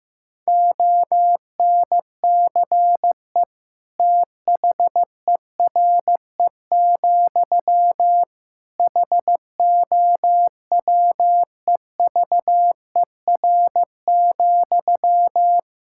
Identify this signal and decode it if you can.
Morse code, heard as a keyed tone, about 16 wpm, 700 Hz. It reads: ONCE THERE, HOWEVER,
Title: Sentences from Top 1000 Words - Review Part 6 - 15wpm